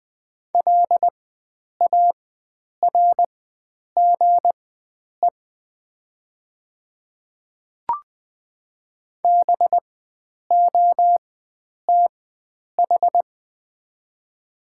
Morse code, heard as a keyed tone, about 20 wpm, spaced as for 9 wpm, 700 Hz. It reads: LARGE BOTH
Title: 4x Character Spacing - Top 500 Words - 20wpm